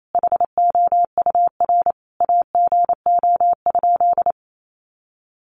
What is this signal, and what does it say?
Morse code, heard as a keyed tone, about 28 wpm, 700 Hz. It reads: HOUR AGO?
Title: Sentences from Top 1300 Words 28wpm